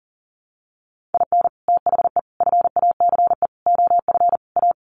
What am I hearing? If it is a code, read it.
Morse code, heard as a keyed tone, about 40 wpm, 700 Hz. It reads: IN THE FACE OF A
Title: Sets of 5 Words Encore 40wpm